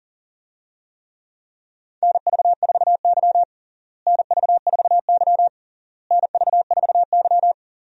Morse code, heard as a keyed tone, about 40 wpm, 700 Hz. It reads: NV4Y NV4Y NV4Y
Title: USA Call Signs - 4 Characters - Repeated 3x - 40wpm